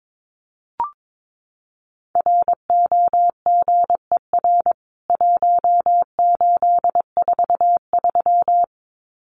Morse code, heard as a keyed tone, about 22 wpm, 700 Hz. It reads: ROGER 1843
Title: CWT Exchange 22wpm